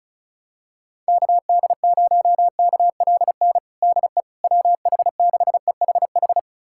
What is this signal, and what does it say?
Morse code, heard as a keyed tone, about 36 wpm, 700 Hz. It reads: KD0KLN DE WH6EHH
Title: USA Callsign DE Callsign - 6 Characters - 35wpm